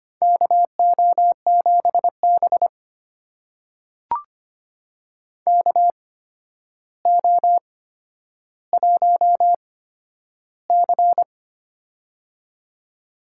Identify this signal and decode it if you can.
Morse code, heard as a keyed tone, about 25 wpm, spaced as for 11 wpm, 700 Hz. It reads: KO7B KO1C
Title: USA Call Signs - 4 Characters - 8x Character Spacing - 25wpm